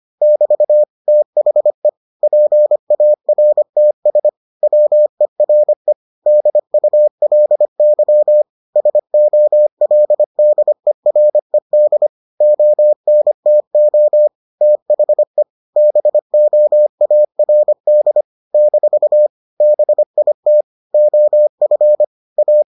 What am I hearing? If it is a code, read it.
Morse code, heard as a keyed tone, about 25 wpm, 600 Hz. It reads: <BT> THE PARTS WERE DULY SOLDERED ONTO THE BOARD - BIT OF A